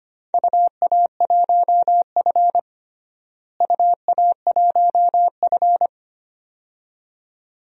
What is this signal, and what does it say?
Morse code, heard as a keyed tone, about 25 wpm, 700 Hz. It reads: UA1F UA1F
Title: DXCC-100 Call Signs - 4 Characters - Repeated 2x - 25wpm